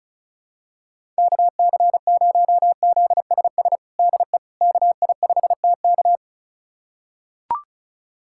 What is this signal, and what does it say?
Morse code, heard as a keyed tone, about 36 wpm, 700 Hz. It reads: KC0ZSS DE KI5TK
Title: USA Callsign DE Callsign 35wpm